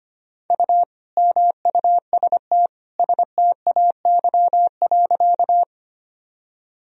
Morse code, heard as a keyed tone, about 25 wpm, 700 Hz. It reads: U MUST STAY.
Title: Sentences from Top 500 Words and Abbreviations 25wpm